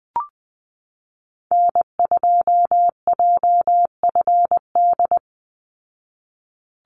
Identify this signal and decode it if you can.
Morse code, heard as a keyed tone, about 20 wpm, 700 Hz. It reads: N2JFD
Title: Call Signs Repeated 2x - 20wpm